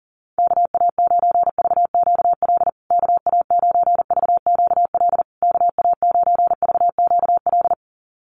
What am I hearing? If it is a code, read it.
Morse code, heard as a keyed tone, about 40 wpm, 700 Hz. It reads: KA9VQL KA9VQL KA9VQL
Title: USA Call Signs Repeated 3x 40wpm